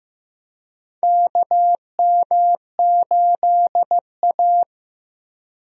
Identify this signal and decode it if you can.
Morse code, heard as a keyed tone, about 15 wpm, 700 Hz. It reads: KM8A
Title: USA Call Signs - 4 Characters - Repeated 3x - 15wpm